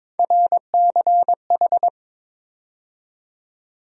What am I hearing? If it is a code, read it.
Morse code, heard as a keyed tone, about 22 wpm, 700 Hz. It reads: RCH
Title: USA Call Sign Postfixes - 3 Letters 22wpm